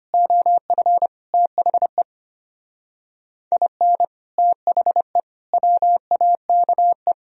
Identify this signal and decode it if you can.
Morse code, heard as a keyed tone, about 29 wpm, 700 Hz. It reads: OF THE IN THE WAKE
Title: Speed-Racing - Sets of 5 Words 25wpm